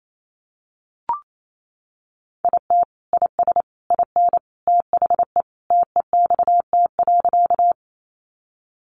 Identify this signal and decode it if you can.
Morse code, heard as a keyed tone, about 28 wpm, 700 Hz. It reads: IT IS IN THE TEXT.
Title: Sentences from Top 1200 Words 28wpm